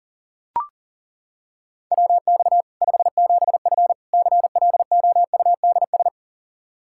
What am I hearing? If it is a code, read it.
Morse code, heard as a keyed tone, about 40 wpm, 700 Hz. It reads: WX 57F CLOUDS
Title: K1USN SST QSO Practice 40wpm